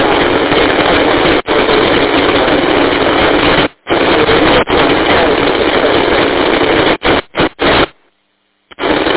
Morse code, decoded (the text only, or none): none